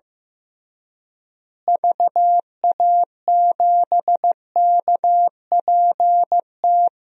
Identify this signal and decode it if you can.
Morse code, heard as a keyed tone, about 15 wpm, 700 Hz. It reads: VA7KPT
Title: DXCC-100 Call Signs - 6 Characters - Repeated 2x - 15wpm